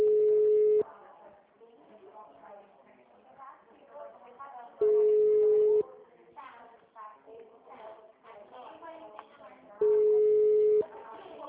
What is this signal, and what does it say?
Ringback tone (Europe, 425 Hz), with speech.
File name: out- -2014- - - .1372